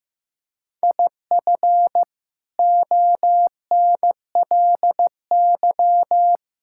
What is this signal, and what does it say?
Morse code, heard as a keyed tone, about 15 wpm, 700 Hz. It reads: IF ONLY
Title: Sentences from Top 1000 Words - Review Part 6 - 15wpm